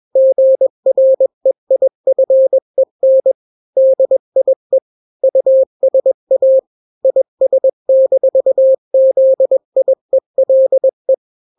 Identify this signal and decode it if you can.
Morse code, heard as a keyed tone, about 21 wpm, 530 Hz. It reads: GREIFEN DIE USA IS-ZIELE
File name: daily / 20wpm